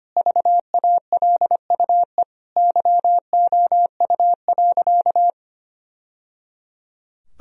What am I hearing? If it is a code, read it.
Morse code, heard as a keyed tone, about 25 wpm, 700 Hz. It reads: VALUE YOU.
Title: Sentences from Top 1000 Words - Review Part 3 - 25wpm